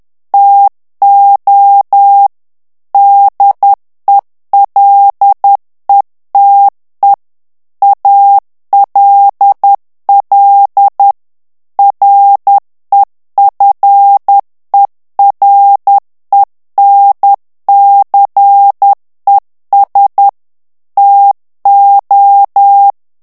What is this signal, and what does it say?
Morse code, heard as a keyed tone, about 10 wpm, 800 Hz. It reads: TO DELETE ALL REFERENCES TO
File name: ANGLAIS texte clair 3 vitesse 10 mots minute